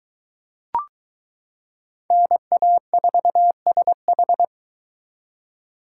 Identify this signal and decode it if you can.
Morse code, heard as a keyed tone, about 22 wpm, 700 Hz. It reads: NA4SH